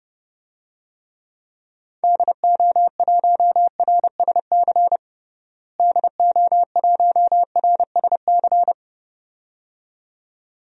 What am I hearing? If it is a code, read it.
Morse code, heard as a keyed tone, about 30 wpm, 700 Hz. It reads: DO1RSC DO1RSC
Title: DXCC-100 Call Signs - 6 Characters - Repeated 2x - 30wpm